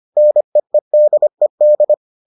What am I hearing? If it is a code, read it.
Morse code, heard as a keyed tone, about 25 wpm, 600 Hz. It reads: NEEDED